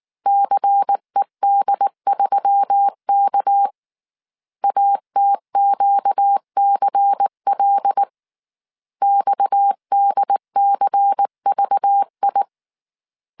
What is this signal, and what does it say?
Morse code, heard as a keyed tone, about 19 wpm, 800 Hz. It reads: CED3K ATQCL XDCVI
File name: CW / DET / 1 6 1500C S C 18wpm 2356